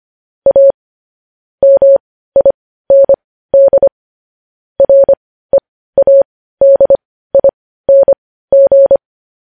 Morse code, heard as a keyed tone, about 25 wpm, spaced as for 15 wpm, 550 Hz. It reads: A MIND READING
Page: News Headlines in Morse Code at 15 WPM